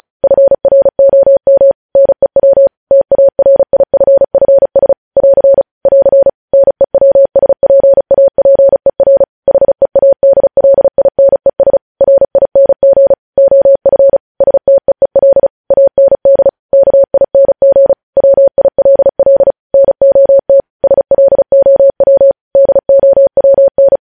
Morse code, heard as a keyed tone, about 36 wpm, 550 Hz. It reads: FROM NEW TARIFFS <AR> <AR> NEWSPAPER HEADLINES RING OF STEEL AND KING WILL NOT SLOW DOWN
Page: News Headlines in Morse Code at 35 WPM